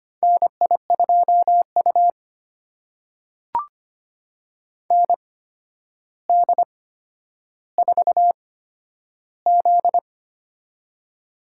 Morse code, heard as a keyed tone, about 25 wpm, spaced as for 11 wpm, 700 Hz. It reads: NI2U ND4Z
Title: USA Call Signs - 4 Characters - 8x Character Spacing - 25wpm